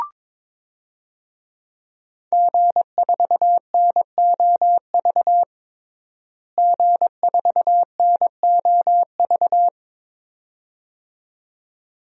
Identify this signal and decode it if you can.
Morse code, heard as a keyed tone, about 21 wpm, 700 Hz. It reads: G4NOV G4NOV
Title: DXCC-100 Call Signs - 5 Characters - Repeated 2x - 22wpm